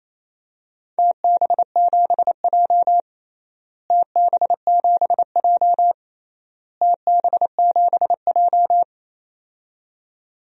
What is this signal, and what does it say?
Morse code, heard as a keyed tone, about 28 wpm, 700 Hz. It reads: TB7J TB7J TB7J
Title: DXCC-100 Call Signs - 4 Characters - Repeated 3x - 28wpm